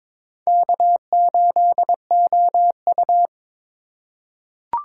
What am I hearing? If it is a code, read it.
Morse code, heard as a keyed tone, about 23 wpm, 700 Hz. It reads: K8OU
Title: USA Call Signs - 4 Characters - 4x Character Spacing - 22wpm